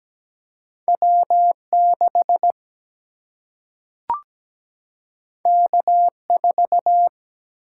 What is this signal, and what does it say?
Morse code, heard as a keyed tone, about 17 wpm, 700 Hz. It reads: W6 K4
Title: Single-Letter Single-Number 17wpm